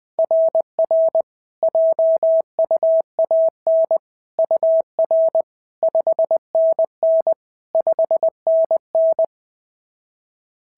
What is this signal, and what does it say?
Morse code, heard as a keyed tone, about 20 wpm, 690 Hz. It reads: RR JUAN UR 5NN 5NN
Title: SOTA QSO Practice 20wpm